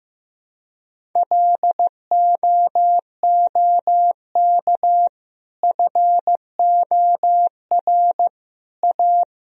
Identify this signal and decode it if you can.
Morse code, heard as a keyed tone, about 15 wpm, 700 Hz. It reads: LOOK FOR A
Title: Sentences from Top 1000 Words - Review Part 2 - 15wpm